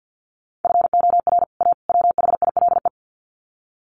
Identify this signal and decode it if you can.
Morse code, heard as a keyed tone, about 50 wpm, 700 Hz. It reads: FOR A WHILE
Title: Sets of 3 Words 50wpm